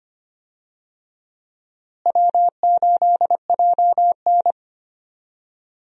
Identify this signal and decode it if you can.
Morse code, heard as a keyed tone, about 25 wpm, 700 Hz. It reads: W8JN
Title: USA Call Signs - 4 Characters - Repeated 2x - 25wpm